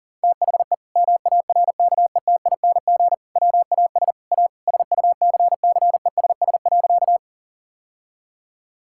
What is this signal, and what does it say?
Morse code, heard as a keyed tone, about 40 wpm, 700 Hz. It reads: THE MARKETING WAS A SUCCESS.
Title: Sentences from Top 1000 Words - Review Part 5 - 40wpm